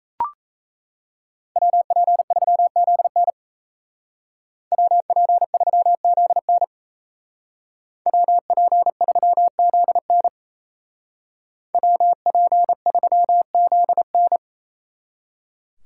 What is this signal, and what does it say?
Morse code, heard as a keyed tone, about 33 wpm, 700 Hz. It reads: WP3ZN WP3ZN WP3ZN WP3ZN
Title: Speed Racing - USA Call Signs 28wpm